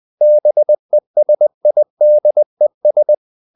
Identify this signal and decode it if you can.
Morse code, heard as a keyed tone, about 20 wpm, 600 Hz. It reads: BESIDES
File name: the old man and the sea part 2 20wpm